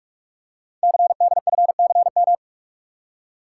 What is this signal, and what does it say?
Morse code, heard as a keyed tone, about 44 wpm, 700 Hz. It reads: CDFCM
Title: Code Groups - 5 Letters - 45wpm